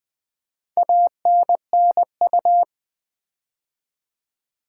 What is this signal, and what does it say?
Morse code, heard as a keyed tone, about 20 wpm, 700 Hz. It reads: ANNU
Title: Build A Word - 4 Letters At A Time - 8-Letter Words - 20wpm